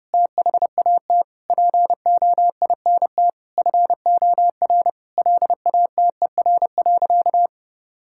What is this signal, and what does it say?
Morse code, heard as a keyed tone, about 30 wpm, 700 Hz. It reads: THAT POINT FOR LATER.
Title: Sentences from Top 1000 Words - Review Part 6 - 30wpm